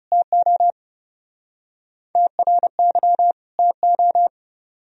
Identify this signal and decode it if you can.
Morse code, heard as a keyed tone, about 39 wpm, 700 Hz. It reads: TO TRY TO